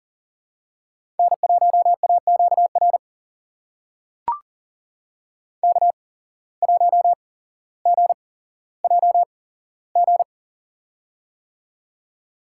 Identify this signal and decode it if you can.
Morse code, heard as a keyed tone, about 40 wpm, spaced as for 17 wpm, 700 Hz. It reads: N1AQR K1GJG
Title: USA Call Signs - 5 Characters - 8x Character Spacing - 40wpm